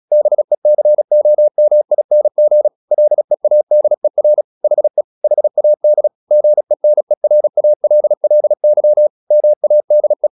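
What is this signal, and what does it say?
Morse code, heard as a keyed tone, about 36 wpm, 600 Hz. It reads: BECOMING LEADER HE HAD GENERALLY MADE
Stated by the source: Morse practice at 36 wpm, text of Ken Follett's novel Never, from CW med Gnister.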